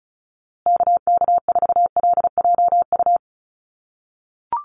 Morse code, heard as a keyed tone, about 34 wpm, 700 Hz. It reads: KK4LJU
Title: USA Call Signs Repeated 2x 35wpm